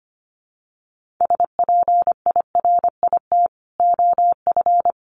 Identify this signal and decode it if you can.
Morse code, heard as a keyed tone, about 25 wpm, 700 Hz. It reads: SPIRIT OF
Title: Sentences from Top 1400 Words 25wpm